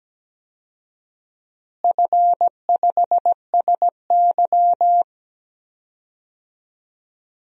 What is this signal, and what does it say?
Morse code, heard as a keyed tone, about 17 wpm, 700 Hz. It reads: F5SY
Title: DXCC-100 Call Signs - 4 Characters - Repeated 3x - 17wpm